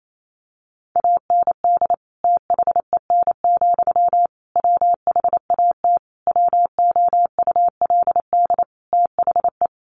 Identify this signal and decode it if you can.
Morse code, heard as a keyed tone, about 28 wpm, 700 Hz. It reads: AND THEN, WHAT WOULD THE
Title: Sentences from Top 1500 Words 28wpm